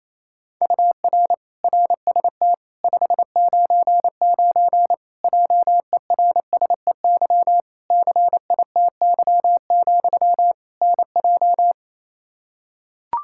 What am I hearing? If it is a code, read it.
Morse code, heard as a keyed tone, about 28 wpm, 700 Hz. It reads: UR RST 599 JERSEY CITY, NJ
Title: Practice QSO - Exchange 1 - Beginning - 28wpm